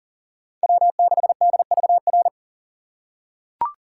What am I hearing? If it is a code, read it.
Morse code, heard as a keyed tone, about 40 wpm, 700 Hz. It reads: W6DVR
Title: USA Call Signs - 4x Character Spacing - 40wpm